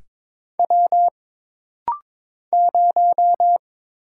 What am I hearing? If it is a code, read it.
Morse code, heard as a keyed tone, about 22 wpm, 700 Hz. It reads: W 0
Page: This rapid-fire timing is geared to push your ICR (Instant Character Recognition) skills.